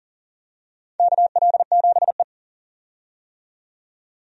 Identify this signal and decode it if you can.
Morse code, heard as a keyed tone, about 37 wpm, 700 Hz. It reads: KL7E
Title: DXCC-100 Call Signs - 4 Characters - 40wpm